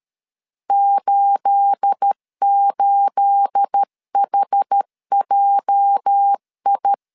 Morse code, heard as a keyed tone, about 13 wpm, 800 Hz. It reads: 88HJI
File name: LEZIONI / 9 28 12wpm 2351